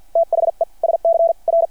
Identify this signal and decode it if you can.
Morse code, heard as a keyed tone, about 48 wpm, spaced as for 40 wpm, 660 Hz. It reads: THE SCR
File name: archived / CRSnet